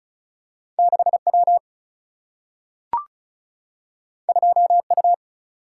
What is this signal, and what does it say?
Morse code, heard as a keyed tone, about 36 wpm, 700 Hz. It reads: 6 W 2 U
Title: Single-Letter Single-Number 35wpm